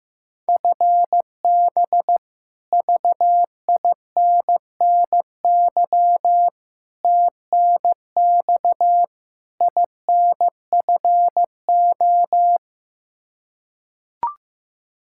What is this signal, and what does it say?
Morse code, heard as a keyed tone, about 15 wpm, 700 Hz. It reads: FB VINNY TNX INFO
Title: Practice QSO - Exchange 3 - More Info - 15wpm